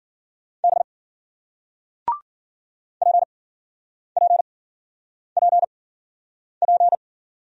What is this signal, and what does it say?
Morse code, heard as a keyed tone, about 56 wpm, spaced as for 9 wpm, 700 Hz. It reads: B PPPP